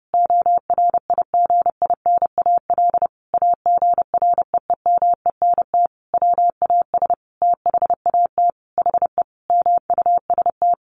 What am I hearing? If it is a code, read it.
Morse code, heard as a keyed tone, about 30 wpm, 700 Hz. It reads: ORIGINAL AGREEMENT WAS THAT HE MUST
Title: Sentences from Top 1500 Words 30wpm